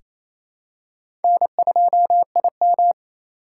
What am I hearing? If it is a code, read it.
Morse code, heard as a keyed tone, about 28 wpm, 700 Hz. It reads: N2IM
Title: USA Call Signs - 4 Characters - 2x Character Spacing - 28wpm